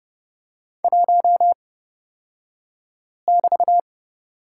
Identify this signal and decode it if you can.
Morse code, heard as a keyed tone, about 29 wpm, spaced as for 5 wpm, 700 Hz. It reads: 1<BT>
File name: Lesson 36d - BT - Two Characters 30wpm